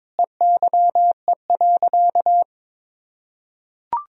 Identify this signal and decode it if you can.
Morse code, heard as a keyed tone, about 22 wpm, 700 Hz. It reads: EYE.
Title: Sentences from Top 1000 Words - Review Part 1 - 22wpm